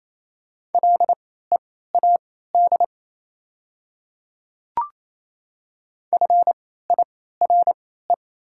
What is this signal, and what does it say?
Morse code, heard as a keyed tone, about 28 wpm, spaced as for 16 wpm, 700 Hz. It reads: LEAD FIRE
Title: 3x Character Spacing - Top 500 Words - 28wpm